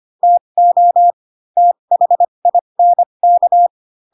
Morse code, heard as a keyed tone, about 25 wpm, spaced as for 22 wpm, 700 Hz. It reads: TO THINK
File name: Pinocchio / Pinocchio ch3 18 25 ik1hge